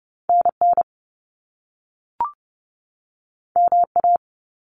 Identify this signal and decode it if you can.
Morse code, heard as a keyed tone, about 30 wpm, 700 Hz. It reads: NN MA
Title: Top Two Letter Combinations in Words 30wpm